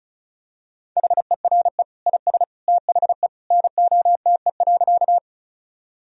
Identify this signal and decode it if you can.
Morse code, heard as a keyed tone, about 36 wpm, 700 Hz. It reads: HERE IS THE NOTE.
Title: Sentences from Top 1000 Words - Review Part 2 - 35wpm